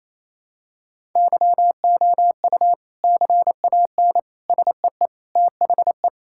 Morse code, heard as a keyed tone, about 28 wpm, 700 Hz. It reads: YOU CAN SEE THE
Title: Sentences from Top 1000 Words - Review Part 5 - 28wpm